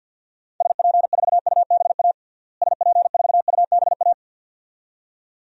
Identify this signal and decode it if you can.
Morse code, heard as a keyed tone, about 49 wpm, 700 Hz. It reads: SP4UBA SP4UBA
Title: DXCC-100 Call Signs - 6 Characters - Repeated 2x - 50wpm